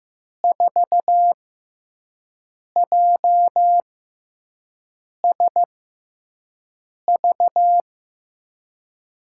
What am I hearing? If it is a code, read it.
Morse code, heard as a keyed tone, about 15 wpm, spaced as for 5 wpm, 700 Hz. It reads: 4JSV
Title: USA Call Signs - 6 Characters - 6x Character Spacing - 15wpm